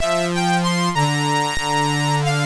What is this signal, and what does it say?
There is no code.